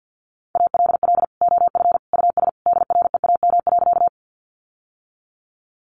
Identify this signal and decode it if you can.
Morse code, heard as a keyed tone, about 48 wpm, 700 Hz. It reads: ALL OF US DREAM.
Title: Sentences from Top 1500 Words 50wpm